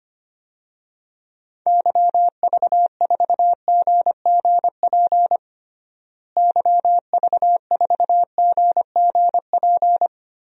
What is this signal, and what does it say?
Morse code, heard as a keyed tone, about 25 wpm, 700 Hz. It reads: YV4GGP YV4GGP
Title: DXCC-100 Call Signs - 6 Characters - Repeated 2x - 25wpm